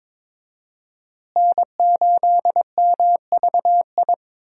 Morse code, heard as a keyed tone, about 22 wpm, 700 Hz. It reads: N8MVI
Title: USA Call Signs - 5 Characters - Repeated 2x - 22wpm